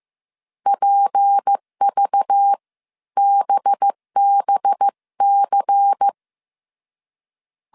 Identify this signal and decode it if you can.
Morse code, heard as a keyed tone, about 15 wpm, 800 Hz. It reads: PV BBC
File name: CW / DET / 1 6 1500C S C 14wpm 2951